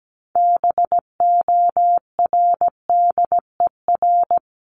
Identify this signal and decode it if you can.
Morse code, heard as a keyed tone, about 17 wpm, 700 Hz. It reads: BORDER
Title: Top 1751-2000 Words 17wpm